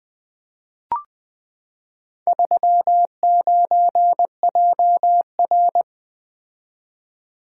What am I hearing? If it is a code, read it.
Morse code, heard as a keyed tone, about 20 wpm, 700 Hz. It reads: 39JR
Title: Four Letter-Number Combinations - 20wpm